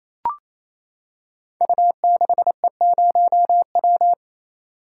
Morse code, heard as a keyed tone, about 28 wpm, 700 Hz. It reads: U6E0W
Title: Code Groups - 5 Letter:Numbers - 28wpm